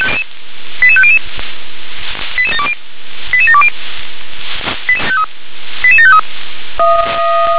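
Five-tone selective call (ZVEI), heard as quick runs of five tones, tones of about 70 ms. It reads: 88500 88720 88753
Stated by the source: Received on SDR as a ZVEI five-tone selective call, address 88126.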